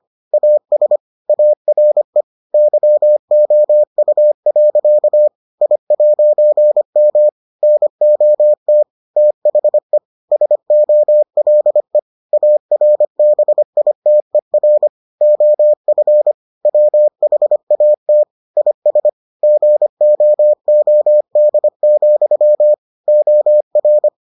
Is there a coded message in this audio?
Yes: Morse code from 0.3 to 24.1 s, heard as a keyed tone, about 25 wpm, 600 Hz. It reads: AS ARE YOU. I'M NOT THE SOLE ARBITER OF WHAT IS GOOD, OR